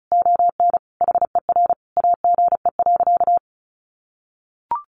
Morse code, heard as a keyed tone, about 34 wpm, 700 Hz. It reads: ON HER AGE.
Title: Sentences from Top 1300 Words 35wpm